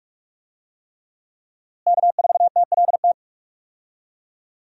Morse code, heard as a keyed tone, about 46 wpm, 700 Hz. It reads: K4TLT
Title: USA Call Signs - 5 Characters - Repeated 3x - 45wpm